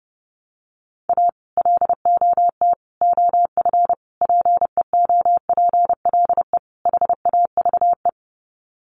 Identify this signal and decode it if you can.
Morse code, heard as a keyed tone, about 30 wpm, 700 Hz. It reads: A LOT OF PEOPLE HAVE
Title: Sets of 5 Words 30wpm